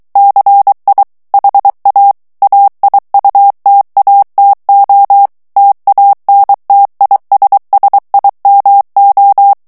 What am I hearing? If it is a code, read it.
Morse code, heard as a keyed tone, about 23 wpm, 800 Hz. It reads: CI HA AIUTATO TANTISSIMO
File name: morse mp3 / ITALIEN texte clair 1 vitesse 22 mots minute